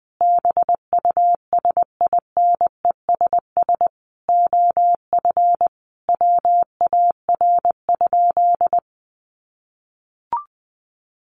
Morse code, heard as a keyed tone, about 20 wpm, 700 Hz. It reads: BUSINESS OF WAR?
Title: Sentences from Top 1300 Words 20wpm